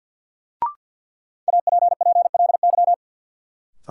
Morse code, heard as a keyed tone, about 52 wpm, 700 Hz. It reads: APPLY
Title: Sentences from Top 1200 Words - New Word Drill - 50wpm